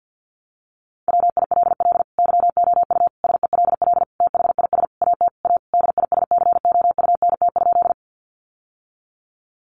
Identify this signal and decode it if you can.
Morse code, heard as a keyed tone, about 50 wpm, 700 Hz. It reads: WILL YOU SELL THIS AT A DISCOUNT?
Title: Sentences from Top 1200 Words 50wpm